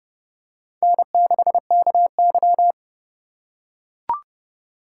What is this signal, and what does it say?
Morse code, heard as a keyed tone, about 28 wpm, 700 Hz. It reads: N6KY